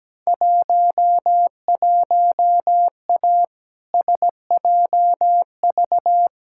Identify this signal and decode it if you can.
Morse code, heard as a keyed tone, about 17 wpm, 700 Hz. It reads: 11A SJV
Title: Field Day Exchanges Encore - 17wpm